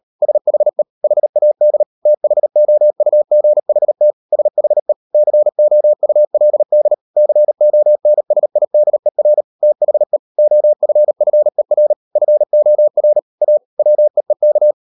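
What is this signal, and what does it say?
Morse code, heard as a keyed tone, about 38 wpm, 600 Hz. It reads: SHE HAD THOUGHT SHE COULD CONSIDER THE OFFER FOR A WEEK
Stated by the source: Morse code practice at 38 WPM, sending Ken Follett's novel Never in English, from CW med Gnister.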